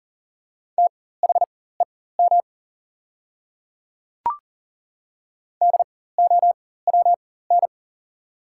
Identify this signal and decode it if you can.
Morse code, heard as a keyed tone, about 40 wpm, spaced as for 19 wpm, 690 Hz. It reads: THEM DOWN
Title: 4x Character Spacing - Top 100 Words - 40wpm